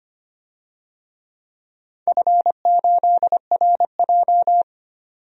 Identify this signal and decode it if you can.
Morse code, heard as a keyed tone, about 25 wpm, 700 Hz. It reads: F8RJ
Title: DXCC-100 Call Signs - 4 Characters - 25wpm